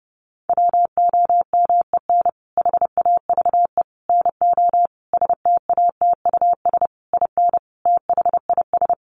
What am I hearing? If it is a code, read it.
Morse code, heard as a keyed tone, about 30 wpm, 700 Hz. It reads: WOMEN HAVE NO STATUS IN THIS
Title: Sentences from Top 1400 Words 30wpm